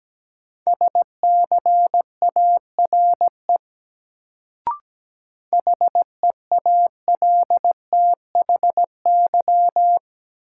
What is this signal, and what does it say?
Morse code, heard as a keyed tone, about 17 wpm, 700 Hz. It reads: SCARE HEALTHY
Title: Sentences from Top 1000 Words - Review Part 5 - New Word Drill - 17wpm